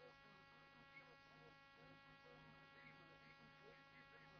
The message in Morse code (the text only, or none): none